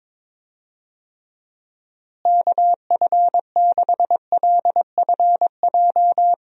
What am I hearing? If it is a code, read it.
Morse code, heard as a keyed tone, about 21 wpm, 700 Hz. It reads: KF6LFJ
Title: USA Call Signs - 6 Characters - Repeated 2x - 22wpm